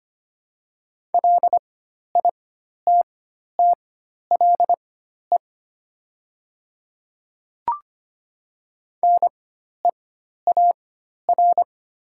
Morse code, heard as a keyed tone, about 25 wpm, spaced as for 12 wpm, 700 Hz. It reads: LITTLE NEAR
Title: 4x Character Spacing - Top 200 Words - 25wpm